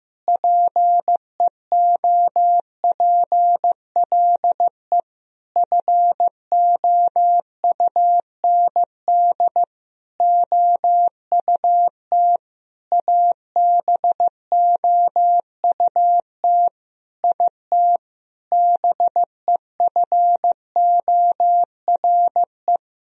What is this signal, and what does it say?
Morse code, heard as a keyed tone, about 15 wpm, 700 Hz. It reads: PEOPLE FOUND OUT ABOUT IT BEFORE
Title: Sentences from Top 1000 Words - Review Part 3 - 15wpm